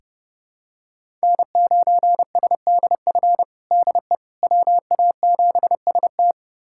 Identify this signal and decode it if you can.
Morse code, heard as a keyed tone, about 32 wpm, 700 Hz. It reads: N9SDF DE WA7ST
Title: USA Callsign DE Callsign - 5 Characters - 30wpm